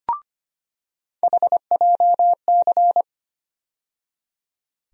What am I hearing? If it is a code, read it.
Morse code, heard as a keyed tone, about 25 wpm, 700 Hz. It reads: HJC
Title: USA Call Sign Postfixes - 3 Letters 25wpm